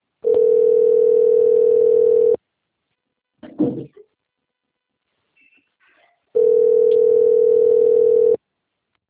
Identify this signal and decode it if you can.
Ringback tone (North America), 440 + 480 Hz.